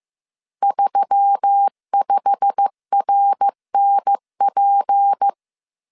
Morse code, heard as a keyed tone, about 15 wpm, 800 Hz. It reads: 35RNP
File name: CW / DET / 1 9 1500C S A 14wpm 3222